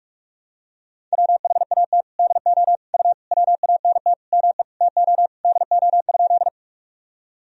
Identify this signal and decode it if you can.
Morse code, heard as a keyed tone, about 42 wpm, 700 Hz. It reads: WHAT DO U WANT ME TO DO?
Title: Sentences from Top 200 Words and Abbreviations 45wpm